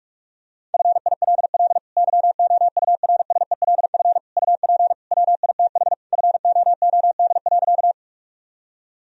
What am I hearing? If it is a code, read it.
Morse code, heard as a keyed tone, about 46 wpm, 700 Hz. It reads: FILL YOURSELF UP WITH FOOD.
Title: Sentences from Top 1000 Words - Review Part 3 - 45wpm